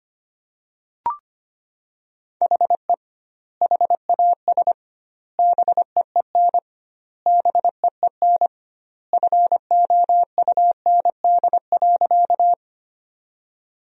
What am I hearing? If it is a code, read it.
Morse code, heard as a keyed tone, about 25 wpm, 700 Hz. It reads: HE HAS BEEN BEEN FOUND.
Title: 2x Word-Spacing - Sentences from Top 300 Words - 25wpm